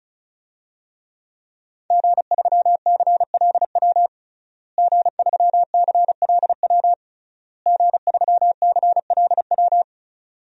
Morse code, heard as a keyed tone, about 36 wpm, 700 Hz. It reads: G3CLW G3CLW G3CLW
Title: DXCC-100 Call Signs - 5 Characters - Repeated 3x - 35wpm